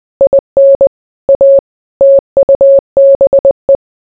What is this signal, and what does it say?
Morse code, heard as a keyed tone, about 20 wpm, 550 Hz. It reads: IN A TUBE